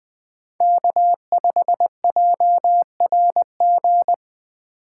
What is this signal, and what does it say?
Morse code, heard as a keyed tone, about 21 wpm, 700 Hz. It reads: K5JRG